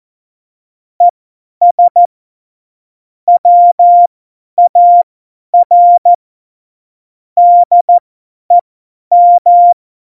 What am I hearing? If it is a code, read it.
Morse code, heard as a keyed tone, about 14 wpm, spaced as for 10 wpm, 700 Hz. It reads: ES WAR DEM
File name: Morsekurs / Witze / Witz 11 Juli 17, Tempo 50 F70